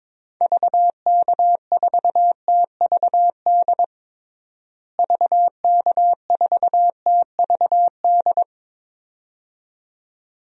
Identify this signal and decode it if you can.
Morse code, heard as a keyed tone, about 22 wpm, 700 Hz. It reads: VK4TVD VK4TVD
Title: DXCC-100 Call Signs - 6 Characters - Repeated 2x - 22wpm